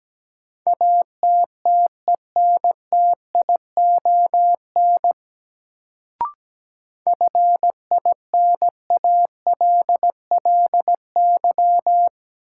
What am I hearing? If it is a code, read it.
Morse code, heard as a keyed tone, about 17 wpm, 700 Hz. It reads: ATTENTION FINALLY
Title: Sentences from Top 1000 Words - New Word Drill - 17wpm